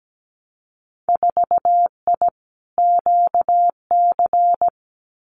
Morse code, heard as a keyed tone, about 17 wpm, 700 Hz. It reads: 4I QC
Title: Winter Field Day Exchanges Encore 17wpm